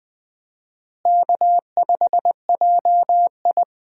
Morse code, heard as a keyed tone, about 20 wpm, 700 Hz. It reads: K5JI
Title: USA Call Signs - 4 Characters - Repeated 3x - 20wpm